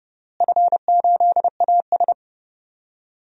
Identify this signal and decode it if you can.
Morse code, heard as a keyed tone, about 32 wpm, 700 Hz. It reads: F8AS
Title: DXCC-100 Call Signs - 4 Characters - Repeated 3x - 30wpm